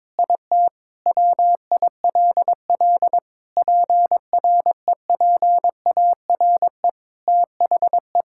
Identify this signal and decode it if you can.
Morse code, heard as a keyed tone, about 22 wpm, 700 Hz. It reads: IT WILL PREPARE THE
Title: Sentences from Top 1000 Words - Review Part 6 - 22wpm